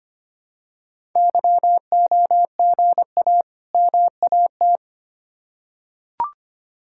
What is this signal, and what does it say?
Morse code, heard as a keyed tone, about 25 wpm, 700 Hz. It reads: YOGA MAT
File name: practice / 25wpm / Difficult Words with Hints - 25wpm